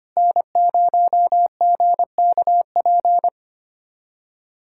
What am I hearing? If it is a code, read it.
Morse code, heard as a keyed tone, about 25 wpm, 700 Hz. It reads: N0GKP
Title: USA Callsign DE Callsign - 5 Characters - 25wpm